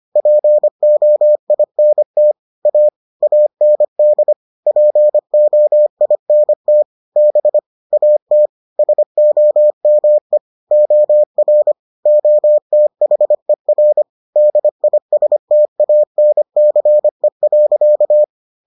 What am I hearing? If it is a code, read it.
Morse code, heard as a keyed tone, about 25 wpm, 600 Hz. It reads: POINT A AND POINT B AT SOME OR OTHER DISTANCE.